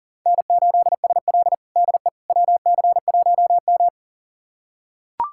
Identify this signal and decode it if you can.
Morse code, heard as a keyed tone, about 40 wpm, 700 Hz. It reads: N8SL DE WC1M
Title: USA Callsign DE Callsign - 4 Characters - 40wpm